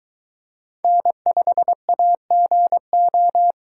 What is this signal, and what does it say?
Morse code, heard as a keyed tone, about 22 wpm, 700 Hz. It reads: N5AGO